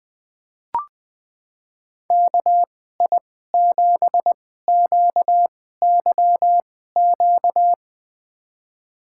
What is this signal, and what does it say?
Morse code, heard as a keyed tone, about 21 wpm, spaced as for 15 wpm, 700 Hz. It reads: KI7QYQ
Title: USA Call Signs - 6 Characters - 2x Character Spacing - 20wpm